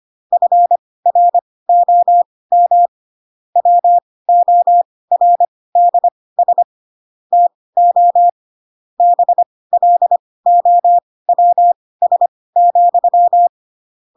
Morse code, heard as a keyed tone, about 25 wpm, spaced as for 18 wpm, 700 Hz. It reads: FROM WORDS TO BLOWS,
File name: Pinocchio / Pinocchio ch2 12 25 ik1hge